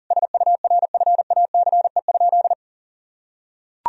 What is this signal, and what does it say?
Morse code, heard as a keyed tone, about 40 wpm, 700 Hz. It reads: SURFACE?
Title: Sentences from Top 1000 Words - Review Part 3 - 40wpm